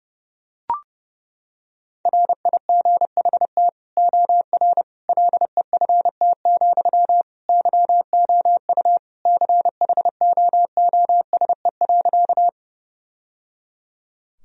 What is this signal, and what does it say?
Morse code, heard as a keyed tone, about 30 wpm, 700 Hz. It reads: RIGHT OR LEFT, YOU CHOOSE.
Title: Sentences from Top 1000 Words - Review Part 4 - 30wpm